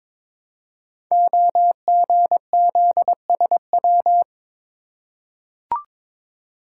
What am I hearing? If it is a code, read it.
Morse code, heard as a keyed tone, about 22 wpm, 700 Hz. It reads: OGZSW
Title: Code Groups - 5 Letter:Numbers - 22wpm